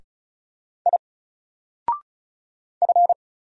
Morse code, heard as a keyed tone, about 36 wpm, 700 Hz. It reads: I F
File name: Single-Letter-Number-35wpm